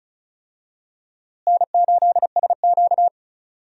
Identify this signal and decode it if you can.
Morse code, heard as a keyed tone, about 36 wpm, 700 Hz. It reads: N8SQ
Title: USA Call Signs - 4 Characters - Repeated 2x - 35wpm